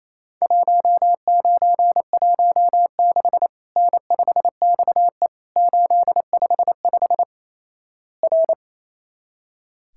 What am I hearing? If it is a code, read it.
Morse code, heard as a keyed tone, about 28 wpm, 700 Hz. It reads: 1916 N5XE 855 R
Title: Worked All Europe DX Contest - QTC Exchange - 28wpm